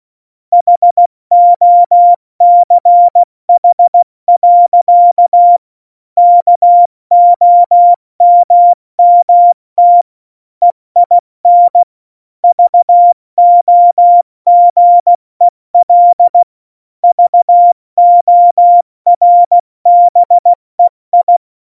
Morse code, heard as a keyed tone, about 16 wpm, 700 Hz. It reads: HOCH. KOMMT EIN VOGEL VORBEI
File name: Morsekurs / Witze / Witz 14. Januar 2020, 70 F80